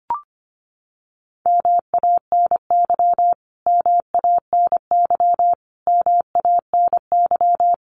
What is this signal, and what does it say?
Morse code, heard as a keyed tone, about 25 wpm, 700 Hz. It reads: MANY MANY MANY
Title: Top 500 Words Repeated 25wpm